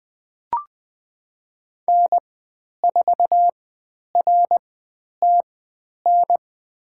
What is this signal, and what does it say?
Morse code, heard as a keyed tone, about 20 wpm, spaced as for 10 wpm, 700 Hz. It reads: N4RTN
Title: 095 Lesson 16g - Letter c - Call Signs 20wpm